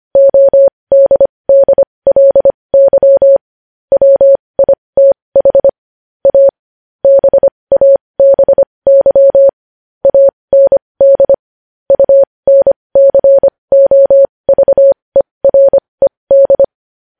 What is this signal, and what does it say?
Morse code, heard as a keyed tone, about 25 wpm, spaced as for 20 wpm, 550 Hz. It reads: ODDLY WITH A BABY AND UNCOVERED
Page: news headlines in morse code at 20 wpm